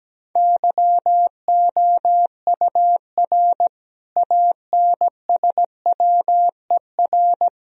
Morse code, heard as a keyed tone, about 17 wpm, 700 Hz. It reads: YOUR ANSWER
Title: Sentences from Top 1000 Words - Review Part 3 - 17wpm